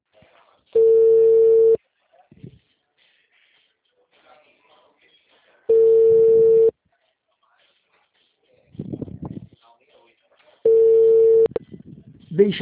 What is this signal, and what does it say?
Ringback tone (China, 450 Hz), with speech.